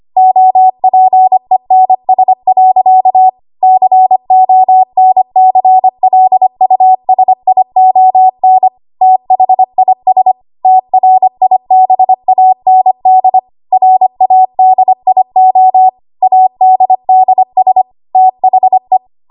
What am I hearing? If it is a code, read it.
Morse code, heard as a keyed tone, about 25 wpm, 750 Hz. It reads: OPENS. CONCLUSION THIS TRIBAND RADIO ADDS THE
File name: Archive / 25 WPM / 25WPM